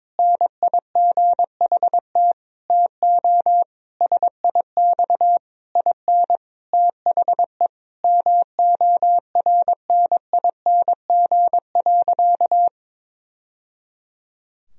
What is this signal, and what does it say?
Morse code, heard as a keyed tone, about 22 wpm, 700 Hz. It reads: NIGHT TO SIX IN THE MORNING.
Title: Sentences from Top 1000 Words - Review Part 2 - 22wpm